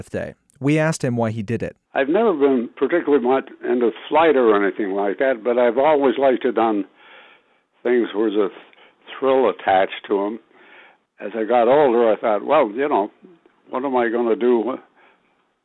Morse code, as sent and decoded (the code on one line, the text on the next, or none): none